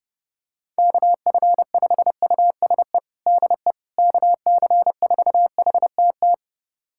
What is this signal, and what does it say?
Morse code, heard as a keyed tone, about 30 wpm, 700 Hz. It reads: KF5USE DE KC4HTT
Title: USA Callsign DE Callsign 30wpm